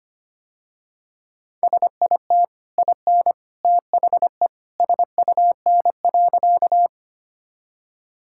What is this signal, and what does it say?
Morse code, heard as a keyed tone, about 25 wpm, 700 Hz. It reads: SIT IN THE SUN.
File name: Easy Sentences with Hints - 25wpm